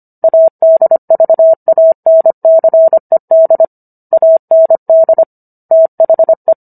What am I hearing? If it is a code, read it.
Morse code, heard as a keyed tone, about 25 wpm, 650 Hz. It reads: ADVANCED AND THE